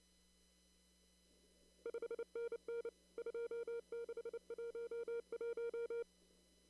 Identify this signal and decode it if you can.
Morse code, heard as a keyed tone, about 29 wpm, 470 Hz. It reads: 5NN 2611
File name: CW / kt5j wpx cw-op-k5pi